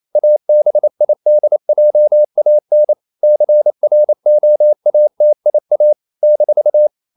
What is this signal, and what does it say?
Morse code, heard as a keyed tone, about 28 wpm, 600 Hz. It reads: ABIDJAN CROATIA -
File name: CW / Practice / NationalCapitals28